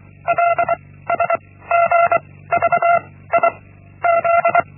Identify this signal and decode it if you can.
Morse code, heard as a keyed tone, about 24 wpm, 680 Hz. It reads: L S G V I Z
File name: Linksmakalnis-1